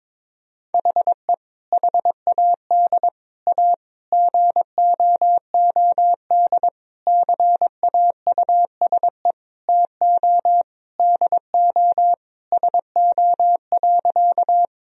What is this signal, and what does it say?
Morse code, heard as a keyed tone, about 22 wpm, 700 Hz. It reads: HE HAD A GOOD CAUSE TO DO SO.
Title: Sentences from Top 1000 Words - Review Part 1 - 22wpm